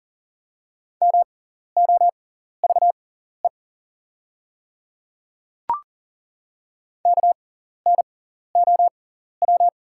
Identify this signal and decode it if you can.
Morse code, heard as a keyed tone, about 40 wpm, spaced as for 14 wpm, 690 Hz. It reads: MOVE KNOW
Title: 6x Character Spacing - Top 200 Words - 40wpm